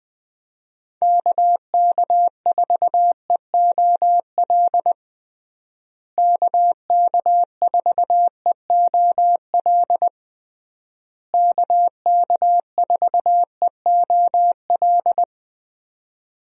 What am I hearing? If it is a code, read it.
Morse code, heard as a keyed tone, about 20 wpm, 700 Hz. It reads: KK4EOL KK4EOL KK4EOL
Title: USA Call Signs - 6 Characters - Repeated 3x - 20wpm